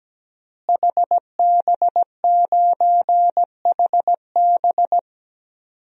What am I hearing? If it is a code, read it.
Morse code, heard as a keyed tone, about 17 wpm, 700 Hz. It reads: HB9HB
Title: DXCC-100 Call Signs - 5 Characters - Repeated 2x - 17wpm